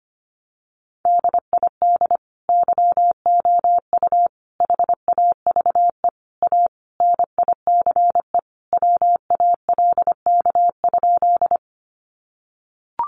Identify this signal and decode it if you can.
Morse code, heard as a keyed tone, about 25 wpm, 700 Hz. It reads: DID YOU HAVE A NICE WALK?
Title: Sentences from Top 1100 Words 25wpm